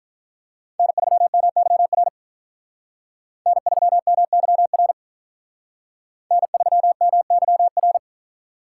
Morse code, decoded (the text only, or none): N3MYR N3MYR N3MYR